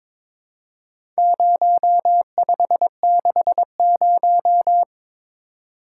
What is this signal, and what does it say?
Morse code, heard as a keyed tone, about 22 wpm, 700 Hz. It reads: 0560
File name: practice / 22wpm / Four Letter-Number Combinations - 22wpm